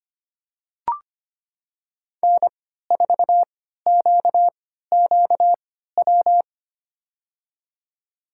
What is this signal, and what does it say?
Morse code, heard as a keyed tone, about 25 wpm, spaced as for 14 wpm, 700 Hz. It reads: N4QQW
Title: USA Call Signs - 5 Characters - 3x Character Spacing - 25wpm